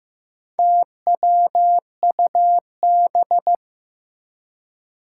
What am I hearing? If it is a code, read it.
Morse code, heard as a keyed tone, about 15 wpm, 700 Hz. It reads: TWUB
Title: Four Challenging Letter-Number Combinations 15wpm